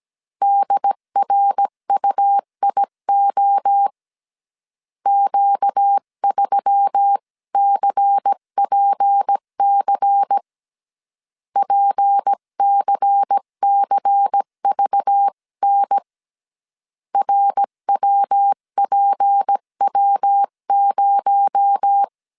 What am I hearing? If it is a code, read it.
Morse code, heard as a keyed tone, about 17 wpm, 800 Hz. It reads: DRUIO Q3CPC PCCVN RWPW0
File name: CW / DET / 1 6 1500C S C 16wpm 2733